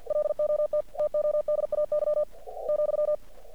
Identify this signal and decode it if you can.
Morse code, heard as a keyed tone, about 47 wpm, 600 Hz. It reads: LOT TODAY ,
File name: archived / CRSnet